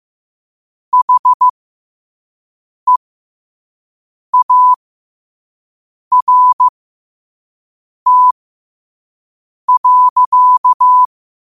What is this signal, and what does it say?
Morse code, heard as a keyed tone, about 15 wpm, spaced as for 5 wpm, 1000 Hz. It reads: HEART.
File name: qotdfiles / 5wpm